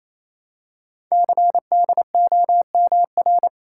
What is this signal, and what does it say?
Morse code, heard as a keyed tone, about 27 wpm, 700 Hz. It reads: CDOMR
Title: Code Groups - 5 Letters - 28wpm